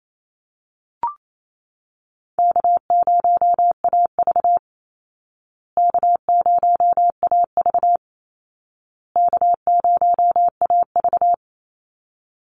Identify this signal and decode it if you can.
Morse code, heard as a keyed tone, about 28 wpm, 700 Hz. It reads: K0AV K0AV K0AV
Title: Call Signs Repeated 2x - 28wpm